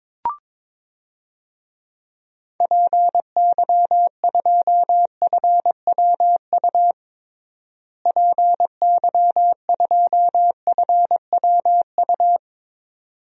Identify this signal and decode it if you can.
Morse code, heard as a keyed tone, about 22 wpm, 700 Hz. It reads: PY2FWU PY2FWU
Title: DXCC-100 Call Signs - 6 Characters - Repeated 2x - 22wpm